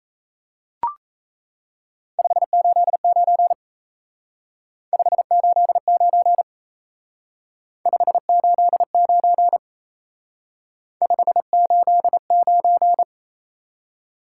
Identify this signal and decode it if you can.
Morse code, heard as a keyed tone, about 35 wpm, 700 Hz. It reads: 589 589 589 589
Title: Speed-Racing Top 180 QSO Elements 28wpm